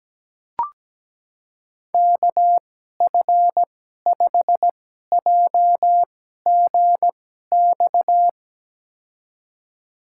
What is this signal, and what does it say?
Morse code, heard as a keyed tone, about 17 wpm, spaced as for 12 wpm, 700 Hz. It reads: KF5JGX
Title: USA Call Signs - 6 Characters - 2x Character Spacing - 17wpm